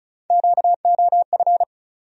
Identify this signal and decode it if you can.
Morse code, heard as a keyed tone, about 36 wpm, 700 Hz. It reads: QOF
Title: USA Call Sign Postfixes - 3 Letters 35wpm